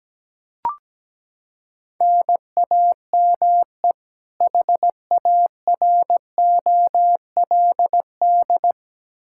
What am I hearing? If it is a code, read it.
Morse code, heard as a keyed tone, about 17 wpm, 700 Hz. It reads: NAME HAROLD
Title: Practice QSO - Exchange 1 - Beginning - 17wpm